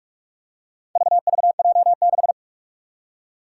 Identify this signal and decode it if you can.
Morse code, heard as a keyed tone, about 46 wpm, 700 Hz. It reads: VVJ6
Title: Four Challenging Letter-Number Combinations 45wpm